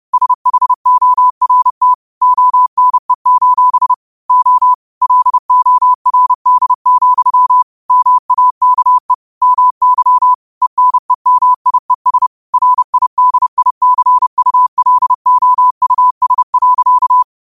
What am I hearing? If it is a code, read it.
Morse code, heard as a keyed tone, about 30 wpm, 1000 Hz. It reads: SHORT ONE: O LORD, MAKE MY ENEMIES RIDICULOUS.
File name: 30wpm